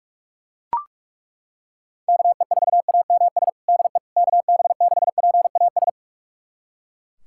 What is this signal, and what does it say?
Morse code, heard as a keyed tone, about 41 wpm, 700 Hz. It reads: KE4AMS DE KB6PAS
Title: USA Callsign DE Callsign 45wpm